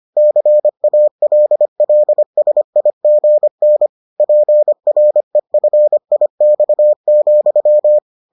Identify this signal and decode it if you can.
Morse code, heard as a keyed tone, about 25 wpm, 600 Hz. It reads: CALLSIGN PREFIX,